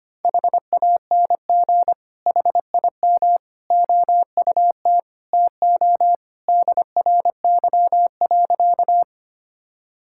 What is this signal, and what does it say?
Morse code, heard as a keyed tone, about 25 wpm, 700 Hz. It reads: HANG HIM OUT TO DRY.
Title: Sentences from Top 1600 Words 25wpm